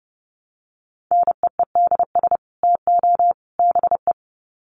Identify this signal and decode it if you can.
Morse code, heard as a keyed tone, about 30 wpm, 700 Hz. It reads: NEEDS TO BE
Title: Sets of 3 Words 30wpm